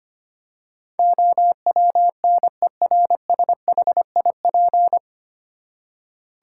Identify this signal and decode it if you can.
Morse code, heard as a keyed tone, about 25 wpm, 700 Hz. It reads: OWNERSHIP
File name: Nine Letter Words - 25wpm